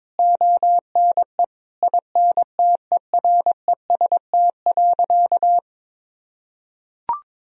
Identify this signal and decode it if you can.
Morse code, heard as a keyed tone, about 22 wpm, 700 Hz. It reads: ONE INTEREST.
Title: Sentences from Top 1000 Words - Review Part 4 - 22wpm